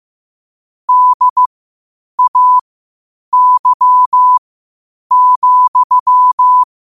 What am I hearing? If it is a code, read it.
Morse code, heard as a keyed tone, about 15 wpm, spaced as for 8 wpm, 1000 Hz. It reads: DAY,
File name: qotdfiles / 8wpm